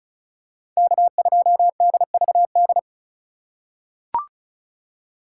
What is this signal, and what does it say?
Morse code, heard as a keyed tone, about 36 wpm, 700 Hz. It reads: K2DVD
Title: USA Call Signs - 5 Characters - 2x Character Spacing - 35wpm